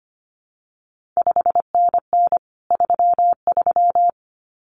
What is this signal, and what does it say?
Morse code, heard as a keyed tone, about 25 wpm, 700 Hz. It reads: 5NN 33
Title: CQ WW DX Contest Exchange Encore - 25wpm